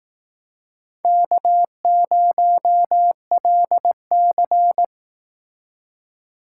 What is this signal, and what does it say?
Morse code, heard as a keyed tone, about 18 wpm, 700 Hz. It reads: K0LC
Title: Speed Racing - USA 1x2 Call Signs 15wpm